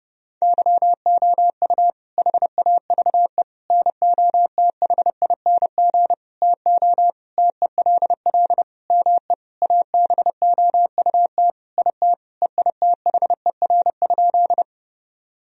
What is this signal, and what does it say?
Morse code, heard as a keyed tone, about 32 wpm, 700 Hz. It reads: YOU HAVE NOTHING TO TELL ME ABOUT IT EITHER?
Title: Sentences from Top 1000 Words - Review Part 4 - 30wpm